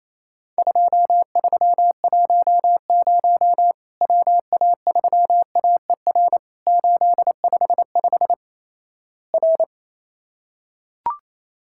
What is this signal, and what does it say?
Morse code, heard as a keyed tone, about 28 wpm, 700 Hz. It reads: 2310 WA3AER 855 R
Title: Worked All Europe DX Contest - QTC Exchange - 28wpm